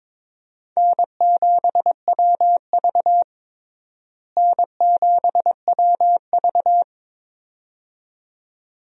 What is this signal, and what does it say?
Morse code, heard as a keyed tone, about 22 wpm, 700 Hz. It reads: N7WV N7WV
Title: USA Call Signs - 4 Characters - Repeated 2x - 22wpm